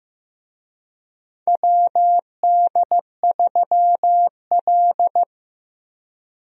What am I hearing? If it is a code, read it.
Morse code, heard as a keyed tone, about 15 wpm, 700 Hz. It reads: WD3L
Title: USA Call Signs - 4 Characters - Repeated 2x - 15wpm